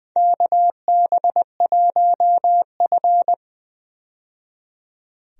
Morse code, heard as a keyed tone, about 20 wpm, 700 Hz. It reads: KB1F
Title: USA Call Signs - 4 Characters - 20wpm